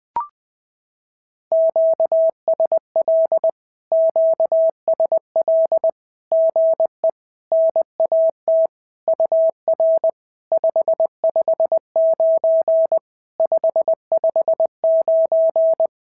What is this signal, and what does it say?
Morse code, heard as a keyed tone, about 21 wpm, 650 Hz. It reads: QSL QSL GE NAT UR 559 559
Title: SOTA QSO Practice 20wpm